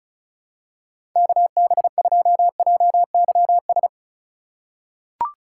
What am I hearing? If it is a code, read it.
Morse code, heard as a keyed tone, about 34 wpm, 690 Hz. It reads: KB2JYS
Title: USA Call Signs - 6 Characters - 2x Character Spacing - 35wpm